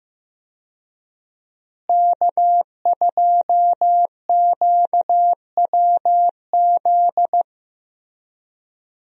Morse code, heard as a keyed tone, about 15 wpm, 700 Hz. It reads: K2QWZ
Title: USA Call Signs - 5 Characters - Repeated 2x - 15wpm